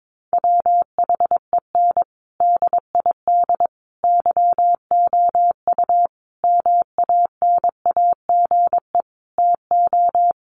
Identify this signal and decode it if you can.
Morse code, heard as a keyed tone, about 22 wpm, 700 Hz. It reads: WHEN DID YOU MANAGE TO
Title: Sentences from Top 1400 Words 22wpm